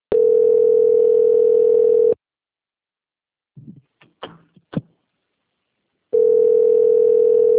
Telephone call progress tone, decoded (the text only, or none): ringback tone